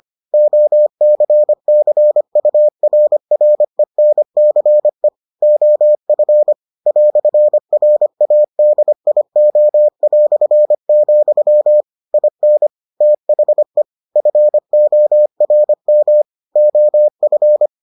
Morse code, heard as a keyed tone, about 25 wpm, 600 Hz. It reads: OCCURRENCE OF "RADIO", IN THE FORM OF